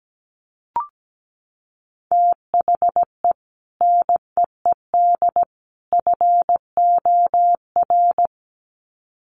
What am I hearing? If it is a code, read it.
Morse code, heard as a keyed tone, about 17 wpm, 700 Hz. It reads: THE NEED FOR
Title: Sets of 3 Words 17wpm